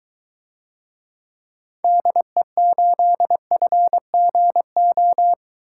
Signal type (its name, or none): Morse code